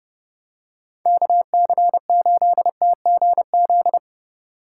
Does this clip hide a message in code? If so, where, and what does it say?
Yes: Morse code from 1.1 to 4.0 s, heard as a keyed tone, about 28 wpm, 700 Hz. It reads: KC8TGZ